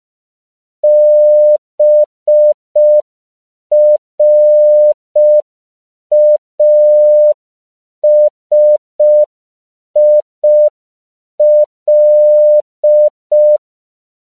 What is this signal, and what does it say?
Morse code, heard as a keyed tone, about 5 wpm, 600 Hz. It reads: BRASIL
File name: CW / 5PPM